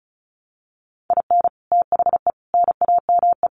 Morse code, heard as a keyed tone, about 34 wpm, 700 Hz. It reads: IN THE NAME
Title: Sets of 5 Words Encore 35wpm